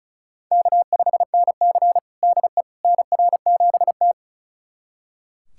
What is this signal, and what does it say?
Morse code, heard as a keyed tone, about 34 wpm, 700 Hz. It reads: K5NC DE NR7T
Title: USA Callsign DE Callsign - 4 Characters - 35wpm